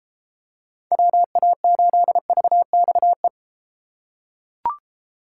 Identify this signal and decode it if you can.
Morse code, heard as a keyed tone, about 32 wpm, 700 Hz. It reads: WA8VXE